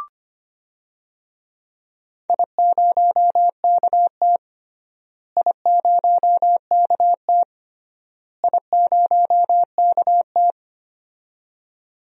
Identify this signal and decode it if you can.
Morse code, heard as a keyed tone, about 25 wpm, 700 Hz. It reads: I0KT I0KT I0KT
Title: DXCC-100 Call Signs - 4 Characters - Repeated 3x - 25wpm